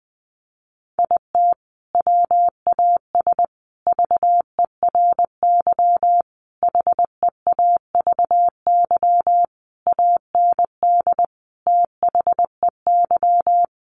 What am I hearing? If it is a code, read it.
Morse code, heard as a keyed tone, about 20 wpm, 700 Hz. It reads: IT WAS VERY HEAVY AND THEY
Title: Sentences from Top 1100 Words 20wpm